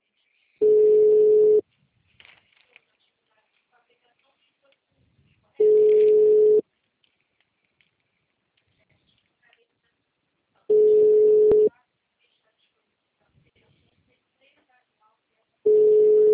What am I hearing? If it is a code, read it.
Ringback tone (Europe, 425 Hz), with speech.